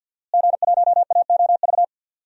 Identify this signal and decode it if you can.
Morse code, heard as a keyed tone, about 51 wpm, 700 Hz. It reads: G1AOV
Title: DXCC-100 Call Signs - 5 Characters - Repeated 3x - 50wpm